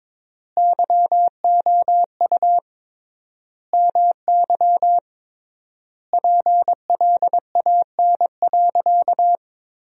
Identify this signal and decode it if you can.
Morse code, heard as a keyed tone, about 22 wpm, 700 Hz. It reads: YOU MY PLAN.
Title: 3x Word-Spacing - Sentences from Top 500 Words - 22wpm